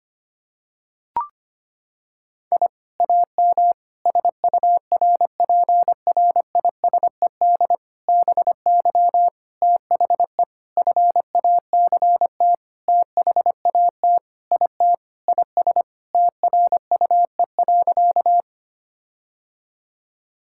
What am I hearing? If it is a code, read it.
Morse code, heard as a keyed tone, about 25 wpm, 700 Hz. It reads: I AM SURPRISED BY THE FACT THAT IT IS TRUE.
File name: practice / 25wpm / Sentences with 4-Word Phrases - 25wpm